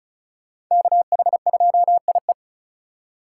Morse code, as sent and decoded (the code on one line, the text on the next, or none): -.- .... ..--- .. .
KH2IE